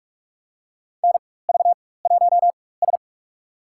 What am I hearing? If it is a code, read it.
Morse code, heard as a keyed tone, about 46 wpm, spaced as for 21 wpm, 700 Hz. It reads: NV1S
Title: USA Call Signs - 4 Characters - 4x Character Spacing - 45wpm